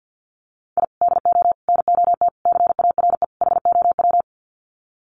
Morse code, heard as a keyed tone, about 48 wpm, 700 Hz. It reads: I DO NOT CARE HOW
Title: Sets of 5 Words Encore2 50wpm